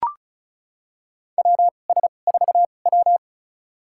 Morse code, heard as a keyed tone, about 36 wpm, spaced as for 25 wpm, 690 Hz. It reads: WS4W
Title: USA Call Signs - 4 Characters - 2x Character Spacing - 35wpm